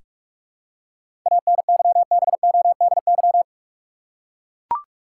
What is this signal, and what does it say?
Morse code, heard as a keyed tone, about 42 wpm, 700 Hz. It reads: ANYBODY
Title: Seven Letter Words - 45wpm